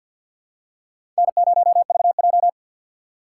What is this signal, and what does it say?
Morse code, heard as a keyed tone, about 48 wpm, 700 Hz. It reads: N0VJ
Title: USA Call Signs - 4 Characters - 6x Character Spacing - 50wpm